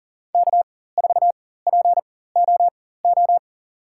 Morse code, heard as a keyed tone, about 40 wpm, spaced as for 19 wpm, 700 Hz. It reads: K4POO
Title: USA Call Signs - 5 Characters - 4x Character Spacing - 40wpm